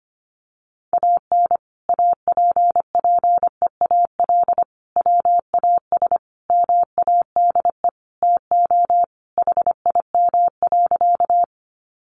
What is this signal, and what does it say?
Morse code, heard as a keyed tone, about 25 wpm, 700 Hz. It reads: AN APPEAL WAS MADE TO HIM.
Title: Sentences from Top 1300 Words 25wpm